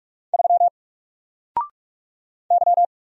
Morse code, heard as a keyed tone, about 44 wpm, 700 Hz. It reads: 3 Y
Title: Single Character (Letter:Number:Punctuation:Prosign) 45wpm